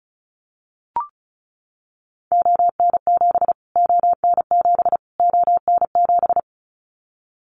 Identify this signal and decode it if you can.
Morse code, heard as a keyed tone, about 34 wpm, 700 Hz. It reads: ON7 ON7 ON7
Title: Callsign Prefixes for Top 25 DXCC - Repeated 3x - 35wpm